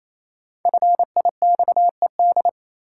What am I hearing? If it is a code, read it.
Morse code, heard as a keyed tone, about 28 wpm, 700 Hz. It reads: FIXED
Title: Sentences from Top 1100 Words - New Word Drill Alt - 28wpm